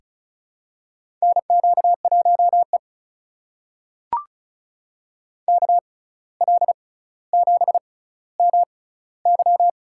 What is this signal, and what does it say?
Morse code, heard as a keyed tone, about 36 wpm, spaced as for 17 wpm, 700 Hz. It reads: NQ1E KL7MY
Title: USA Call Signs - 6x Character Spacing - 35wpm